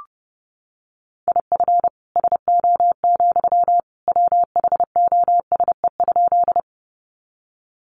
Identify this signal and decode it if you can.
Morse code, heard as a keyed tone, about 30 wpm, 700 Hz. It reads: IF SO, WHOSE?
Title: Sentences from Top 1200 Words 30wpm